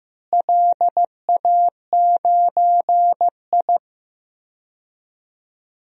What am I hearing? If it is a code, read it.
Morse code, heard as a keyed tone, about 15 wpm, 700 Hz. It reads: LA9I
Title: DXCC-100 Call Signs - 4 Characters - Repeated 3x - 15wpm